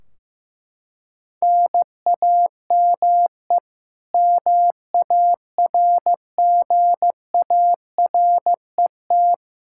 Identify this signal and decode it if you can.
Morse code, heard as a keyed tone, about 15 wpm, 700 Hz. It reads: NAME MARGARET
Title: K1USN SST QSO Practice 15wpm